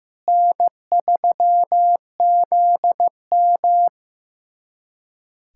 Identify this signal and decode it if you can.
Morse code, heard as a keyed tone, about 15 wpm, 700 Hz. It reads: N3ZM
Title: USA Callsign DE Callsign - 4 Characters - 15wpm